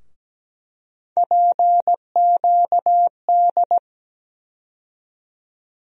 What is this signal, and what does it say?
Morse code, heard as a keyed tone, about 17 wpm, 700 Hz. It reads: PQD
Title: USA Call Sign Postfixes - 3 Letters 17wpm